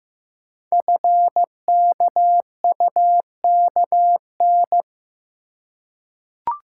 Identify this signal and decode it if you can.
Morse code, heard as a keyed tone, about 15 wpm, 700 Hz. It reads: FKUKN
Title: Code Groups - 5 Letters - 15wpm